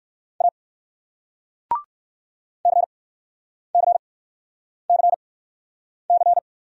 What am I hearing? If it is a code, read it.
Morse code, heard as a keyed tone, about 56 wpm, spaced as for 9 wpm, 700 Hz. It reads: A CCCC
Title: Speed-Racing Single Letter-Number 45wpm